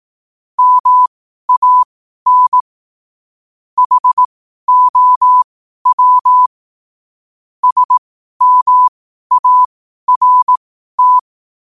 Morse code, heard as a keyed tone, about 18 wpm, spaced as for 12 wpm, 1000 Hz. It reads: MAN HOW SMART
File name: qotdfiles / 12wpm